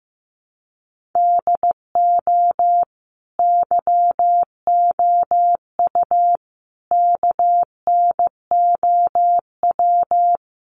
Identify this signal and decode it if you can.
Morse code, heard as a keyed tone, about 15 wpm, 700 Hz. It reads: DO YOU KNOW
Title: Sentences from Top 100 Words 15wpm